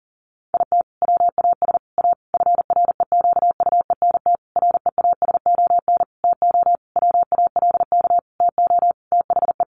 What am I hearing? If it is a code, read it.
Morse code, heard as a keyed tone, about 40 wpm, 700 Hz. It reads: IT WAS A FREQUENT REASON TO WALK TO THE
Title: Sentences from Top 1400 Words 40wpm